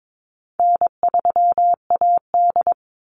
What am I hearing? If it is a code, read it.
Morse code, heard as a keyed tone, about 22 wpm, 700 Hz. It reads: N3AD
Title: Easy Call Signs Repeated 3x - 22wpm